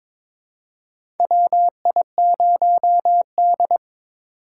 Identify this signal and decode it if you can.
Morse code, heard as a keyed tone, about 22 wpm, 700 Hz. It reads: WI0D
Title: USA Call Signs - 4 Characters - 3x Character Spacing - 22wpm